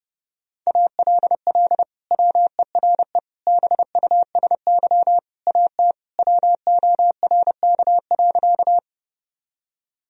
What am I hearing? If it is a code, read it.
Morse code, heard as a keyed tone, about 30 wpm, 700 Hz. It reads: ALL WERE BUSY AT WORK.
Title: Sentences from Top 1000 Words - Review Part 3 - 30wpm